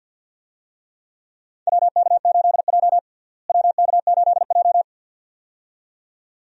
Morse code, heard as a keyed tone, about 52 wpm, 700 Hz. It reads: WK8J WK8J
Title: USA Call Signs - 4 Characters - Repeated 2x - 50wpm